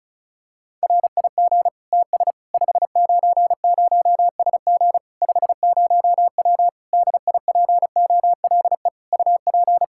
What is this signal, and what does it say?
Morse code, heard as a keyed tone, about 36 wpm, 700 Hz. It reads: RIG TS 590SG 50W DIPOLE UP
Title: Practice QSO - Exchange 2 - Info - 35wpm